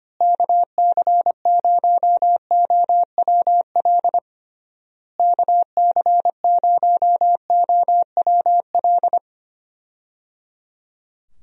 Morse code, heard as a keyed tone, about 25 wpm, 700 Hz. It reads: KC0OWL KC0OWL
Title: USA Call Signs - 6 Characters - Repeated 3x - 25wpm